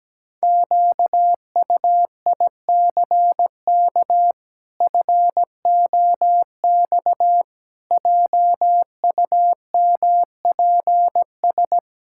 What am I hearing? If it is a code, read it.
Morse code, heard as a keyed tone, about 17 wpm, 700 Hz. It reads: QUICK FOX JUMPS
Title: Pangram Challenge v2 - 17wpm